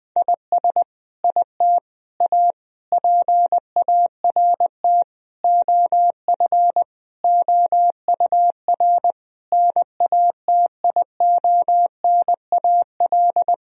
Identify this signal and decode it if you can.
Morse code, heard as a keyed tone, about 20 wpm, 700 Hz. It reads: IS IT A PART OF OUR NATIONAL
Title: Sentences from Top 1000 Words - Review Part 4 - 20wpm